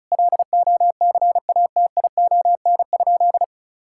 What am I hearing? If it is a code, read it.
Morse code, heard as a keyed tone, about 36 wpm, 700 Hz. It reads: LOCATION?
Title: Sentences from Top 1000 Words - Review Part 6 - 35wpm